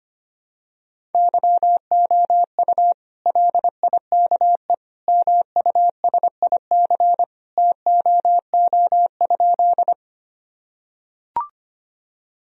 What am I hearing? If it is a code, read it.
Morse code, heard as a keyed tone, about 25 wpm, 700 Hz. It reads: YOU LIKE MUSIC TOO?
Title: Sentences from Top 1000 Words - Review Part 1 - 25wpm